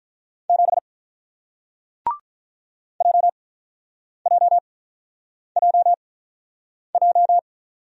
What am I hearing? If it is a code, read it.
Morse code, heard as a keyed tone, about 47 wpm, spaced as for 9 wpm, 700 Hz. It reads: 7 JJJJ